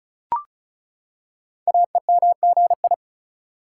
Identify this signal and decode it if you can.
Morse code, heard as a keyed tone, about 36 wpm, 700 Hz. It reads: AEMGI
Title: Code Groups - 5 Letters - 35wpm